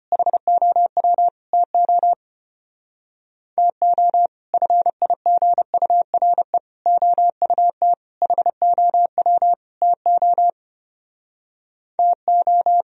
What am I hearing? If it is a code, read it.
Morse code, heard as a keyed tone, about 33 wpm, 700 Hz. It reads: HOW TO TO FIGURE OUT HOW TO TO